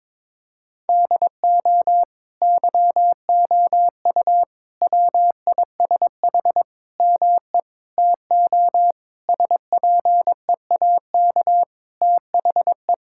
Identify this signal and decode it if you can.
Morse code, heard as a keyed tone, about 22 wpm, 700 Hz. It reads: DO YOU WISH ME TO SPEAK THE
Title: Sentences from Top 1600 Words 22wpm